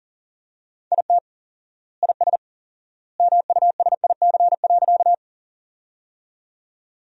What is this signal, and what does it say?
Morse code, heard as a keyed tone, about 40 wpm, 700 Hz. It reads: IT IS MUSIC.
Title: 4x Word-Spacing - Sentences from Top 300 Words - 40wpm